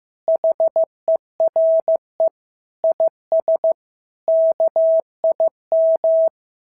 Morse code, heard as a keyed tone, about 15 wpm, 650 Hz. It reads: HERE IS KIM
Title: SOTA QSO Practice 15wpm